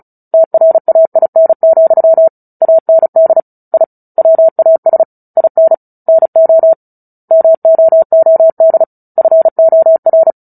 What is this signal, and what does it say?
Morse code, heard as a keyed tone, about 34 wpm, 650 Hz. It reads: TRAIN, AND I WAS IN NO MOOD FOR
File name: Book Two / 09-Chapter 9 - 35 WPM